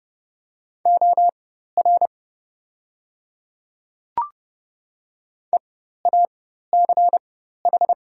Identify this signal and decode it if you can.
Morse code, heard as a keyed tone, about 30 wpm, spaced as for 14 wpm, 700 Hz. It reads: OR EACH
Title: 4x Character Spacing - Top 100 Words - 30wpm